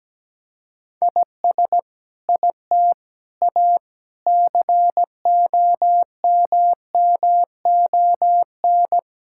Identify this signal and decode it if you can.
Morse code, heard as a keyed tone, about 17 wpm, 700 Hz. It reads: IS IT A COMMON
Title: Sentences from Top 1000 Words - Review Part 5 - 17wpm